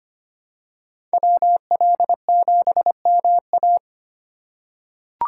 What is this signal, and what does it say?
Morse code, heard as a keyed tone, about 25 wpm, 700 Hz. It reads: WL7MA
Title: USA Call Signs - 5 Characters - 4x Character Spacing - 25wpm